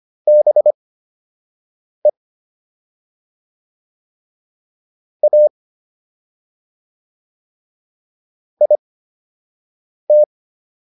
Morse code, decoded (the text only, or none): BE A IT